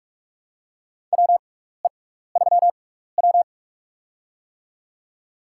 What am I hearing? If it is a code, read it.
Morse code, heard as a keyed tone, about 46 wpm, spaced as for 16 wpm, 690 Hz. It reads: WE3W
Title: USA Call Signs - 6x Character Spacing - 45wpm